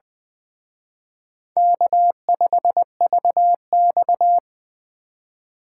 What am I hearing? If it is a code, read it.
Morse code, heard as a keyed tone, about 24 wpm, 700 Hz. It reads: K5VX
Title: Speed Racing - USA 1x2 Call Signs 20wpm